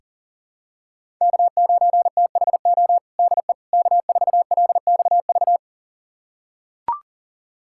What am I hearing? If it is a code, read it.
Morse code, heard as a keyed tone, about 37 wpm, 700 Hz. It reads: K9THO DE K4LXV
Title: USA Callsign DE Callsign 40wpm